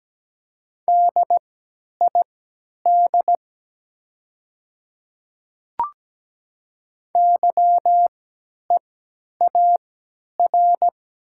Morse code, heard as a keyed tone, about 17 wpm, spaced as for 10 wpm, 700 Hz. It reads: DID YEAR
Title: 3x Character Spacing - Top 500 Words - 17wpm